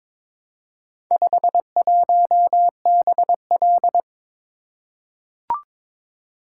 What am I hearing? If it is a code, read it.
Morse code, heard as a keyed tone, about 22 wpm, 700 Hz. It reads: 51BL
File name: practice / 22wpm / Four Letter-Number Combinations - 22wpm